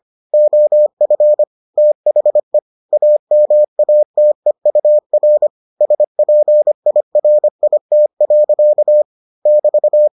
Morse code, heard as a keyed tone, about 25 wpm, 600 Hz. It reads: OF THE AMATEUR SPIRIT. <BT>